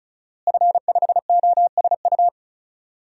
Morse code, heard as a keyed tone, about 34 wpm, 700 Hz. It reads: F5OSU
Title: DXCC-100 Call Signs - 5 Characters - Repeated 2x - 35wpm